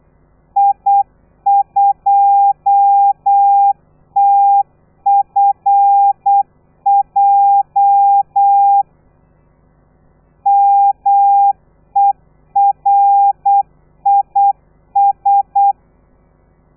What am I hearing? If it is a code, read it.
Morse code, heard as a keyed tone, about 8 wpm, 800 Hz. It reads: I2TFJ MERIS
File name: telegrafia / i2tfj / Lez13 35